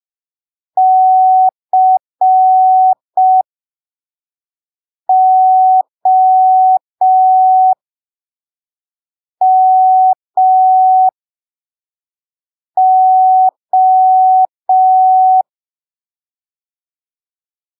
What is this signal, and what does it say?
Morse code, heard as a keyed tone, about 5 wpm, spaced as for 3 wpm, 750 Hz. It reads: COMO
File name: arquivos / texto / texto 18 05ppm